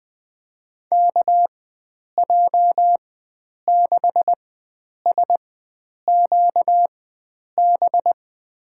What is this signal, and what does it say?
Morse code, heard as a keyed tone, about 20 wpm, spaced as for 9 wpm, 700 Hz. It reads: KJ6SQB
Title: USA Call Signs - 6 Characters - 4x Character Spacing - 20wpm